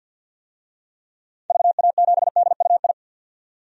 Morse code, heard as a keyed tone, about 46 wpm, 700 Hz. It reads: VA7DUI
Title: DXCC-100 Call Signs - 6 Characters - 50wpm